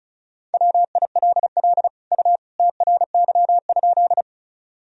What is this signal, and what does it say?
Morse code, heard as a keyed tone, about 36 wpm, 700 Hz. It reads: WILL U TRY?
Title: Sentences from Top 200 Words and Abbreviations 35wpm